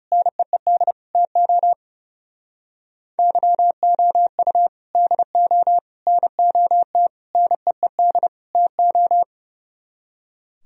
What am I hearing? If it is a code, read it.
Morse code, heard as a keyed tone, about 37 wpm, 700 Hz. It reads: NEED TO YOU DO NOT NEED TO
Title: Speed-Racing - Sets of 5 Words 30wpm